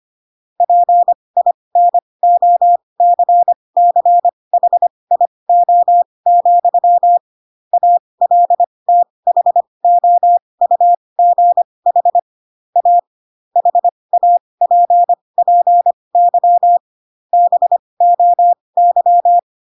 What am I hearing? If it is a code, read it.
Morse code, heard as a keyed tone, about 25 wpm, spaced as for 20 wpm, 700 Hz. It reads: PINOCCHIO, ALTHOUGH A HAPPY BOY
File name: Pinocchio / Pinocchio ch8 15 25 ik1hge